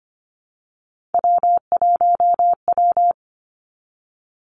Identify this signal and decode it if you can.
Morse code, heard as a keyed tone, about 25 wpm, 700 Hz. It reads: W1W
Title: Three Challenging Letter-Number Combinations 25wpm